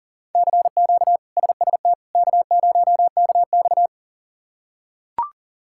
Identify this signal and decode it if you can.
Morse code, heard as a keyed tone, about 40 wpm, 700 Hz. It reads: CQ SST K0KX
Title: K1USN SST QSO Practice - 40wpm